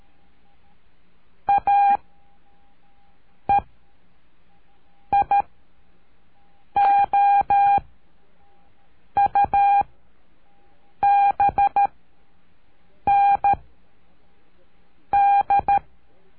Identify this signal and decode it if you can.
Morse code, heard as a keyed tone, about 13 wpm, 800 Hz. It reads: A E I O U B N D
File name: cursos / CURSO DE TELEGRAFIA / Telegrafia - Trilha 03